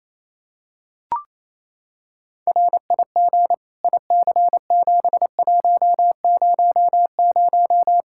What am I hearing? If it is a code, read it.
Morse code, heard as a keyed tone, about 28 wpm, 700 Hz. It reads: RIG IC7100
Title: K1USN SST QSO Practice 28wpm